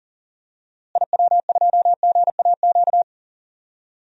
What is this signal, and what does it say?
Morse code, heard as a keyed tone, about 39 wpm, 700 Hz. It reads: IW2GAQ
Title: DXCC-100 Call Signs - 6 Characters - Repeated 3x - 40wpm